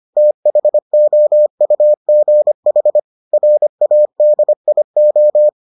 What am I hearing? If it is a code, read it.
Morse code, heard as a keyed tone, about 25 wpm, 600 Hz. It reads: THOUGH RADIO